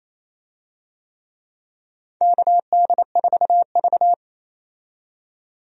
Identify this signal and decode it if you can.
Morse code, heard as a keyed tone, about 27 wpm, 700 Hz. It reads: KD4V
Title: USA Call Signs - 4 Characters - 28wpm